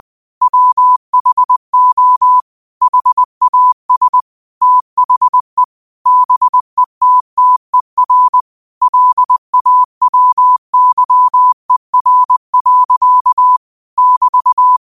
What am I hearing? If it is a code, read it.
Morse code, heard as a keyed tone, about 20 wpm, 1000 Hz. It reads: WHO HAS THE BETTER LAWYER. <BT>